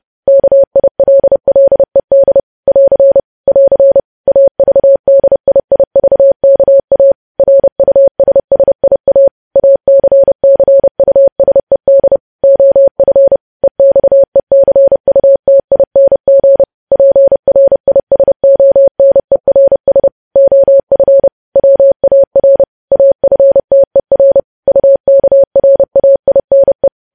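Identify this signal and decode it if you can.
Morse code, heard as a keyed tone, about 30 wpm, 550 Hz. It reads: KILLED <AR> <AR> AVDIIVKA RUSSIA ACCUSED OF EXECUTING PRISONERS OF WAR AFTER UKRAINE
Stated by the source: Morse code practice of news headlines sent at 30 wpm.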